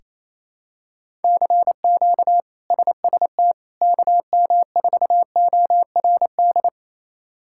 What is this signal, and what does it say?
Morse code, heard as a keyed tone, about 28 wpm, 700 Hz. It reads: CQ SST KM4ORD
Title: K1USN SST QSO Practice - 28wpm